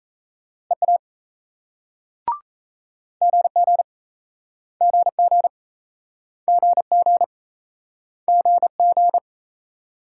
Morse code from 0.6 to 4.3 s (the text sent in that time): EA GG